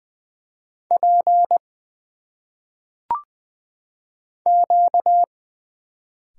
Morse code, heard as a keyed tone, about 20 wpm, 700 Hz. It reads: P Q
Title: Single Letters - Phonetic Pronunciation 20wpm